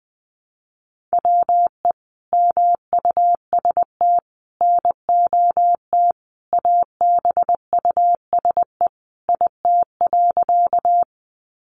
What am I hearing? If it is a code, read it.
Morse code, heard as a keyed tone, about 20 wpm, 700 Hz. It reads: WE MUST NOT ABUSE IT.
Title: Sentences from Top 1300 Words 20wpm